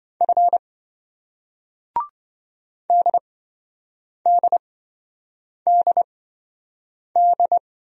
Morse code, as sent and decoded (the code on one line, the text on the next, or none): ..-. / -.. -.. -.. -..
F DDDD